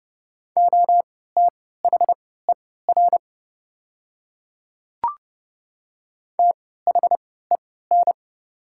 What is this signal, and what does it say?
Morse code, heard as a keyed tone, about 30 wpm, spaced as for 17 wpm, 690 Hz. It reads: OTHER THEN
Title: 3x Character Spacing - Top 200 Words - 30wpm